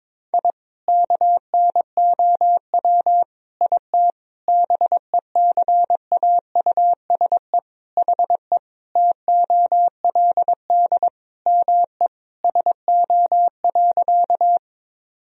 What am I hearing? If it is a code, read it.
Morse code, heard as a keyed tone, about 22 wpm, 700 Hz. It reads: I KNOW IT BECAUSE HE TOLD ME SO.
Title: Sentences from Top 1000 Words - Review Part 3 - 22wpm